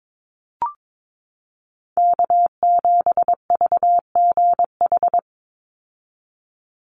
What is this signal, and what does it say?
Morse code, heard as a keyed tone, about 22 wpm, 700 Hz. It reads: K7VGH
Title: USA Call Signs 22wpm